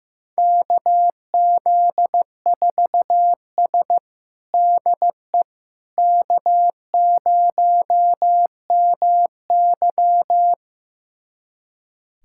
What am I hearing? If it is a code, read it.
Morse code, heard as a keyed tone, about 15 wpm, 700 Hz. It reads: KZ4S DE K0MY
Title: USA Callsign DE Callsign - 4 Characters - 15wpm